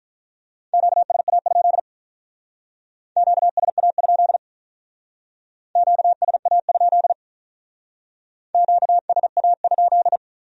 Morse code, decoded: QSA? QSA? QSA? QSA?

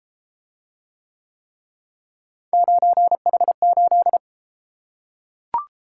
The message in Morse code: 9H8

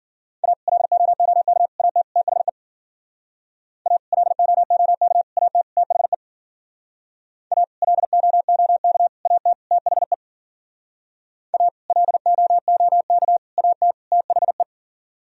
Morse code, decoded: A LOOK AT THE A LOOK AT THE A LOOK AT THE A LOOK AT THE

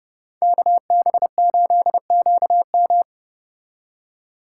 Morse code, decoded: KB8QM

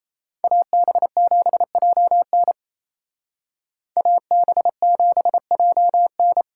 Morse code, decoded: AB7JN AB7JN

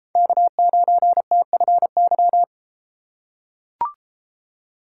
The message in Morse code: K9TFY

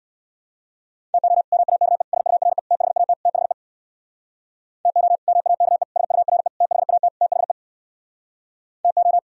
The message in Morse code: TO MAKE SURE THAT THE TO MAKE SURE THAT THE TO

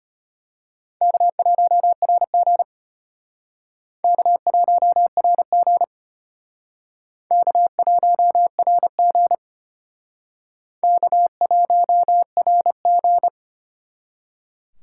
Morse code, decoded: K1RG K1RG K1RG K1RG

